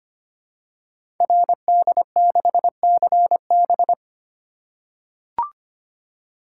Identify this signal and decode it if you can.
Morse code, heard as a keyed tone, about 25 wpm, 700 Hz. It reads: RD6CB
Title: Code Groups - 5 Letter:Numbers - 25wpm